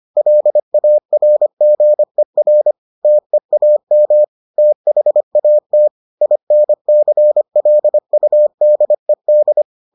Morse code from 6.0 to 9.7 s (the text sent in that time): INCLUDED